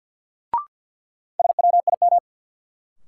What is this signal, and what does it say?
Morse code, heard as a keyed tone, about 52 wpm, 700 Hz. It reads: SWIM
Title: Sentences from Top 1000 Words - Review Part 6 - New Word Drill - 50wpm